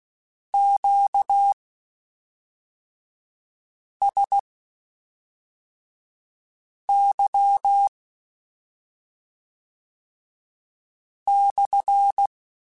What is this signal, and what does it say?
Morse code, heard as a keyed tone, about 16 wpm, spaced as for 3 wpm, 800 Hz. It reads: QSY/
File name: Q Calls / Q Calls-mp3s / Q Calls SPSL1 @5